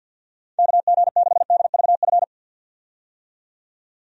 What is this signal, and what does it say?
Morse code, heard as a keyed tone, about 50 wpm, 700 Hz. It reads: KG6DVF
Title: USA Call Signs - 6 Characters - 50wpm